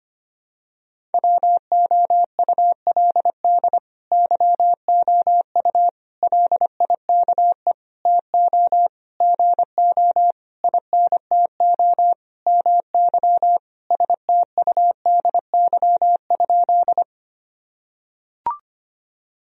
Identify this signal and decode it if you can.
Morse code, heard as a keyed tone, about 25 wpm, 700 Hz. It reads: WOULD YOU LIKE TO GO INTO MY STUDY?
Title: Sentences from Top 1000 Words - Review Part 1 - 25wpm